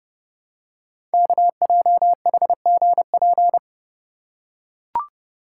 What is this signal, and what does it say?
Morse code, heard as a keyed tone, about 30 wpm, 700 Hz. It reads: KJHGP
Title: Code Groups - 5 Letters - 30wpm